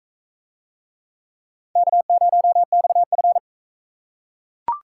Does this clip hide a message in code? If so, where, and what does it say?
Yes: Morse code from 1.8 to 3.4 s, heard as a keyed tone, about 38 wpm, 700 Hz. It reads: K0XF